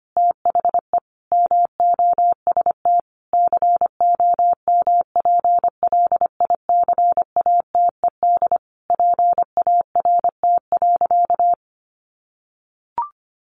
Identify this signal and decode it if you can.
Morse code, heard as a keyed tone, about 25 wpm, 700 Hz. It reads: THE MOST COMPLICATED PART.
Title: Sentences from Top 1200 Words 25wpm